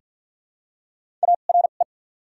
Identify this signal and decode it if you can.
Morse code, heard as a keyed tone, about 48 wpm, spaced as for 35 wpm, 690 Hz. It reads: ARE
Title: 2x Character Spacing - Top 100 Words - 50wpm